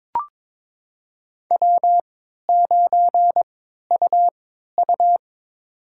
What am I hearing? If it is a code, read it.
Morse code, heard as a keyed tone, about 22 wpm, spaced as for 12 wpm, 700 Hz. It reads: W9UU
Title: USA Call Signs - 4 Characters - 3x Character Spacing - 22wpm